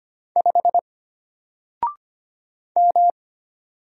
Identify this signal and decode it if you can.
Morse code, heard as a keyed tone, about 25 wpm, 700 Hz. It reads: 5 M
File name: Single-Letter-Number-25wpm